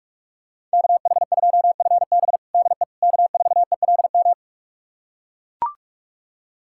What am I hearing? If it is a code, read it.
Morse code, heard as a keyed tone, about 42 wpm, 700 Hz. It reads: KH2FB DE K4ELM
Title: USA Callsign DE Callsign - 5 Characters - 45wpm